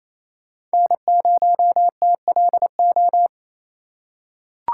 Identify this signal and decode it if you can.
Morse code, heard as a keyed tone, about 27 wpm, 700 Hz. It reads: N0TLO
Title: USA Call Signs - 6x Character Spacing - 28wpm